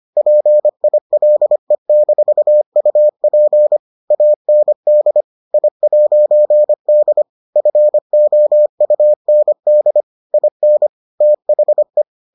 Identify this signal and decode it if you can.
Morse code, heard as a keyed tone, about 25 wpm, 600 Hz. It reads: PILE-UP AND I'D FOUND IN THE